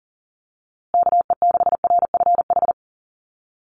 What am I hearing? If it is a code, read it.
Morse code, heard as a keyed tone, about 38 wpm, 700 Hz. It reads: KE6RFH
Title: USA Call Signs Repeated 3x 40wpm